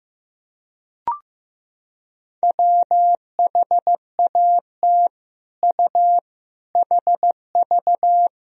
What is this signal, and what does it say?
Morse code, heard as a keyed tone, about 15 wpm, 700 Hz. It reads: WHAT U HV
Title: Sentences from Top 200 Words and Abbreviations 15wpm